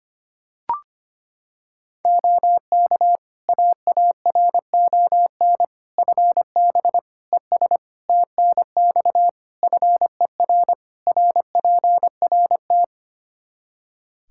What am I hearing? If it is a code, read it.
Morse code, heard as a keyed tone, about 25 wpm, 700 Hz. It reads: OK AARON FB ES TNX FER RPRT
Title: K1USN SST QSO Practice 25wpm